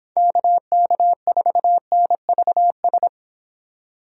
Morse code, heard as a keyed tone, about 24 wpm, 700 Hz. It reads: KK4NVS